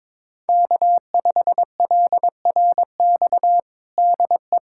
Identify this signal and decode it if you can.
Morse code, heard as a keyed tone, about 22 wpm, 700 Hz. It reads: K5LRX DE
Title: USA Callsign DE Callsign 22wpm